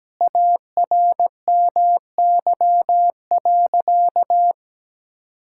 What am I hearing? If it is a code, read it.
Morse code, heard as a keyed tone, about 17 wpm, 700 Hz. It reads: ARMY.
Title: Sentences from Top 1000 Words - Review Part 6 - 17wpm